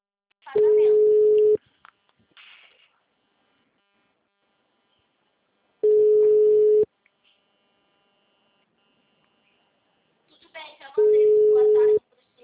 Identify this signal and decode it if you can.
Ringback tone (Europe, 425 Hz), with speech.